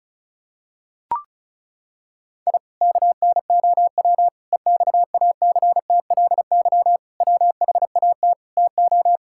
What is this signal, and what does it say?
Morse code, heard as a keyed tone, about 34 wpm, 700 Hz. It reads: I KNOW EXACTLY WHAT TO
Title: Sentences from Top 1000 Words - Review Part 6 - 35wpm